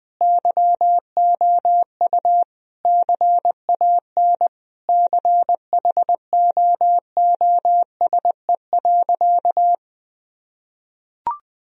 Morse code, heard as a keyed tone, about 20 wpm, 700 Hz. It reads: YOU CAN CHOOSE.
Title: Sentences from Top 1000 Words - Review Part 4 - 20wpm